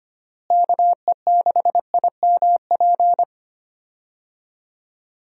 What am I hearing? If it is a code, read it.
Morse code, heard as a keyed tone, about 25 wpm, 700 Hz. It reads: KE6IMP
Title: USA Call Signs - 6 Characters - Repeated 3x - 25wpm